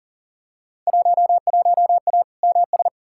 Morse code, heard as a keyed tone, about 40 wpm, 700 Hz. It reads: 11A MS
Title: Field Day Exchanges Encore - 40wpm